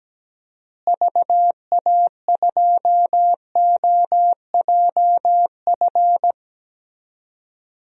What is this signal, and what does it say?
Morse code, heard as a keyed tone, about 17 wpm, 700 Hz. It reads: VA2OJF
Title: DXCC-100 Call Signs - 6 Characters - Repeated 3x - 17wpm